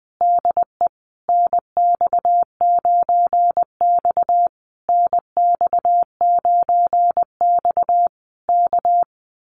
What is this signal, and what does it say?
Morse code, heard as a keyed tone, about 20 wpm, 700 Hz. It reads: DE NX9X NX9X K
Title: Common QSO Phrases Encore 20wpm